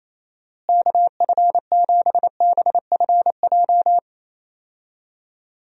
Morse code, heard as a keyed tone, about 28 wpm, 700 Hz. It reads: KF7BFJ
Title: USA Call Signs - 6 Characters - Repeated 2x - 28wpm